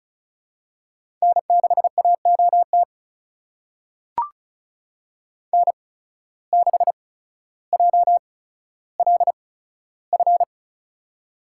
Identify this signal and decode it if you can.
Morse code, heard as a keyed tone, about 36 wpm, spaced as for 15 wpm, 700 Hz. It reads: N6AOT N6JLF
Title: USA Call Signs - 5 Characters - 8x Character Spacing - 35wpm